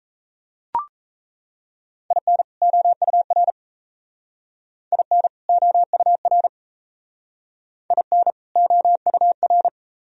Morse code, heard as a keyed tone, about 35 wpm, 700 Hz. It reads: IN OUR IN OUR IN OUR
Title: Speed-Racing - Sets of 2 Words 28wpm